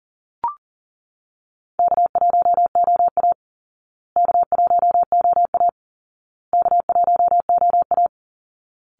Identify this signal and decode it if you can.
Morse code, heard as a keyed tone, about 40 wpm, 700 Hz. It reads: K1OA K1OA K1OA
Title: Easy Call Signs Repeated 3x - 40wpm